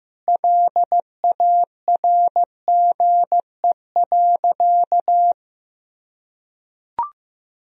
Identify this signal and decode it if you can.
Morse code, heard as a keyed tone, about 15 wpm, 700 Hz. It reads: LARGE.
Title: Sentences from Top 1000 Words - Review Part 1 - 15wpm